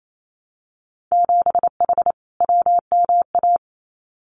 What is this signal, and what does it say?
Morse code, heard as a keyed tone, about 28 wpm, 700 Hz. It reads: 7H WMA
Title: Winter Field Day Exchanges Encore 28wpm